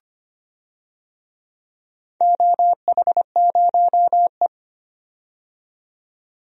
Morse code, heard as a keyed tone, about 25 wpm, 700 Hz. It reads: OH0E
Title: DXCC-100 Call Signs - 4 Characters - 25wpm